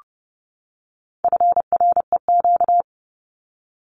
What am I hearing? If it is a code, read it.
Morse code, heard as a keyed tone, about 30 wpm, 700 Hz. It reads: FREQ
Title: Abbreviations 30wpm